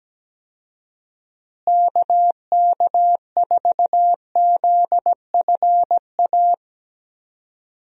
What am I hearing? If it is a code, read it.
Morse code, heard as a keyed tone, about 17 wpm, 700 Hz. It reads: KK4ZFA
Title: USA Call Signs - 6 Characters - Repeated 3x - 17wpm